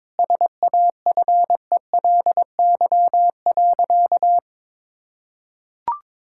Morse code, heard as a keyed tone, about 22 wpm, 700 Hz. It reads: SAFELY.
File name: practice / 22wpm / Easy Sentences with Hints - 22wpm